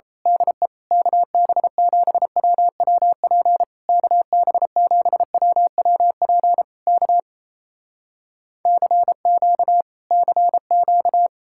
Morse code, heard as a keyed tone, about 27 wpm, 700 Hz. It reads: DE KB7WWP KB7WWP K CQ CQ
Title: Speed-Racing Common QSO Phrases - 28wpm